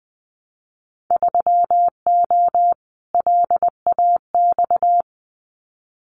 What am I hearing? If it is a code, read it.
Morse code, heard as a keyed tone, about 20 wpm, 700 Hz. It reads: 3O LAX
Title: Winter Field Day Exchanges Encore 20wpm